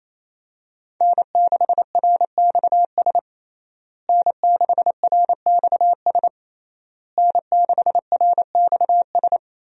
Morse code, heard as a keyed tone, about 28 wpm, 700 Hz. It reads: N6RXS N6RXS N6RXS
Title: USA Call Signs - 5 Characters - Repeated 3x - 28wpm